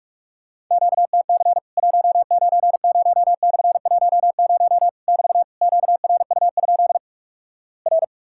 Morse code, heard as a keyed tone, about 46 wpm, 700 Hz. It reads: QTC 190/10 <BT> QRU? R
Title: Worked All Europe DX Contest - QTC Exchange - 45wpm